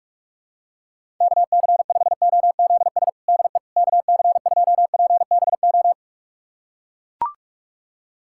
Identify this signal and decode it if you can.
Morse code, heard as a keyed tone, about 42 wpm, 700 Hz. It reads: KC5OZS DE KC2PBO
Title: USA Callsign DE Callsign 45wpm